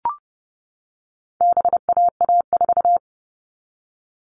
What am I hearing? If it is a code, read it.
Morse code, heard as a keyed tone, about 30 wpm, 700 Hz. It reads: BAA4
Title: All Cut Numbers - 4-Digit 30wpm